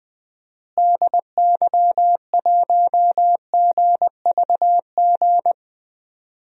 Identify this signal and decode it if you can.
Morse code, heard as a keyed tone, about 21 wpm, 700 Hz. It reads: DY1GVG
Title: DXCC-100 Call Signs - 6 Characters - Repeated 2x - 20wpm